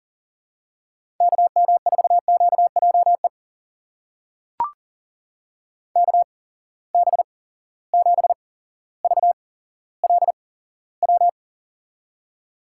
Morse code, heard as a keyed tone, about 39 wpm, spaced as for 17 wpm, 700 Hz. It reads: KM4QJE KB7VLW
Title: USA Call Signs - 6 Characters - 8x Character Spacing - 40wpm